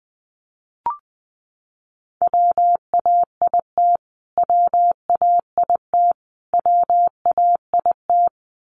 Morse code, heard as a keyed tone, about 20 wpm, 700 Hz. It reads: WAIT WAIT WAIT
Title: Top 500 Words Repeated 20wpm